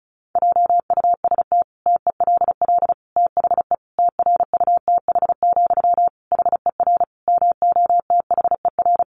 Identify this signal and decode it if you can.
Morse code, heard as a keyed tone, about 34 wpm, 700 Hz. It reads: JUST TELL THE TRUTH, HER MOTHER
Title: Sentences from Top 800 Words 35wpm